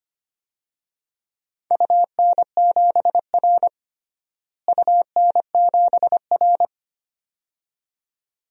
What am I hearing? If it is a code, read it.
Morse code, heard as a keyed tone, about 25 wpm, 700 Hz. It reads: UN7R UN7R
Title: DXCC-100 Call Signs - 4 Characters - Repeated 2x - 25wpm